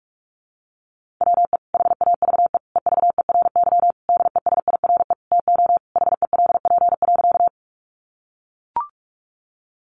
Morse code, heard as a keyed tone, about 44 wpm, 700 Hz. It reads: WE HAVE EVERY DESIRE TO HELP.
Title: Sentences from Top 1100 Words 45wpm